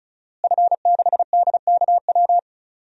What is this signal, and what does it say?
Morse code, heard as a keyed tone, about 34 wpm, 700 Hz. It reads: F6DKW
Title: DXCC-100 Call Signs - 5 Characters - Repeated 3x - 35wpm